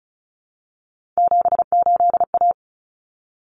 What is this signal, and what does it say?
Morse code, heard as a keyed tone, about 34 wpm, 700 Hz. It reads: 78A
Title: Common Cut Numbers - 3-Digit 35wpm